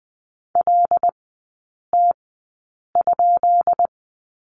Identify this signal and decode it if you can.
Morse code, heard as a keyed tone, about 19 wpm, spaced as for 8 wpm, 700 Hz. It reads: LT?
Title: 133 Lesson 22e - Question Mark - Three Characters 20wpm